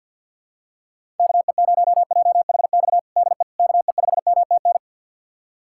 Morse code, heard as a keyed tone, about 52 wpm, 690 Hz. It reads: KE0JHX DE KE5MTN
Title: USA Callsign DE Callsign - 6 Characters - 50wpm